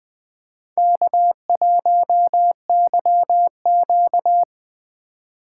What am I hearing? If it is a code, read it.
Morse code, heard as a keyed tone, about 20 wpm, 700 Hz. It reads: K1YQ
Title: USA Call Signs - 4 Characters - 4x Character Spacing - 20wpm